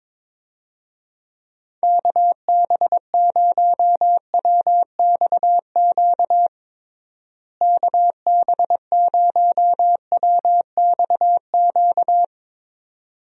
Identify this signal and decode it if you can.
Morse code, heard as a keyed tone, about 22 wpm, 700 Hz. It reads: KB0WXQ KB0WXQ
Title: USA Call Signs - 6 Characters - Repeated 2x - 22wpm